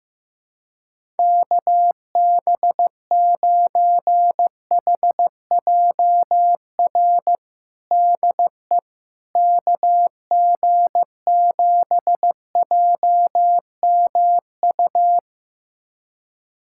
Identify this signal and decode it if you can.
Morse code, heard as a keyed tone, about 15 wpm, 700 Hz. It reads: KB9HJR DE KG7JMU
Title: USA Callsign DE Callsign - 6 Characters - 15wpm